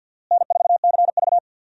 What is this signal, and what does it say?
Morse code, heard as a keyed tone, about 51 wpm, 700 Hz. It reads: N4CV
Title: USA Call Signs - 4 Characters - Repeated 3x - 50wpm